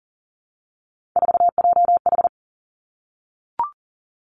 Morse code, heard as a keyed tone, about 42 wpm, 700 Hz. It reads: 4JH
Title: Three Challenging Letter-Number Combinations 40wpm